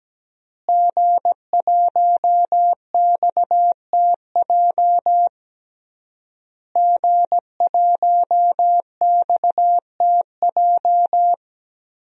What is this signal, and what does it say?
Morse code, heard as a keyed tone, about 17 wpm, 700 Hz. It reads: G1XTJ G1XTJ
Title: DXCC-100 Call Signs - 5 Characters - Repeated 3x - 17wpm